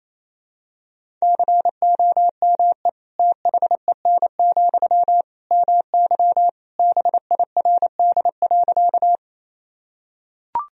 Morse code, heard as a keyed tone, about 28 wpm, 700 Hz. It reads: COME THEN, MY BIRD.
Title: Sentences from Top 1000 Words - Review Part 2 - 28wpm